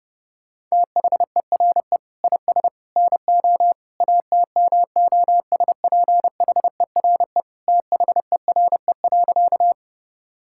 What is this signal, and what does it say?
Morse code, heard as a keyed tone, about 30 wpm, 700 Hz. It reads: THERE IS NO ATMOSPHERE THERE.
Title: Sentences from Top 1600 Words 30wpm